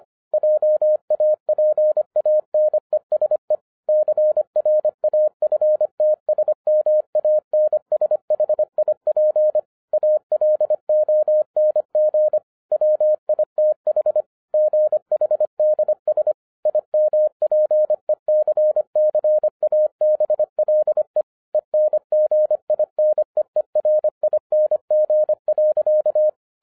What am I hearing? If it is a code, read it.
Morse code, heard as a keyed tone, about 25 wpm, 610 Hz. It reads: JAPANESE CRAFTSMANSHIP ALONG WITH GHDS IMPECCABLE ENGINEERING.
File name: 2011-10-mp3 / n14-MORSE EXPRESS 2011 CHRISTMAS KEY 25wpm